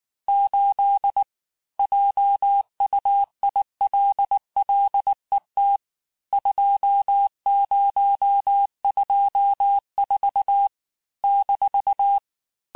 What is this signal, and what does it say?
Morse code, heard as a keyed tone, about 19 wpm, 800 Hz. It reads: 8 JUILLET 2024 -
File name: cours-du-lundi files / cours-1589-texte-chiffre-vitesse-18mpm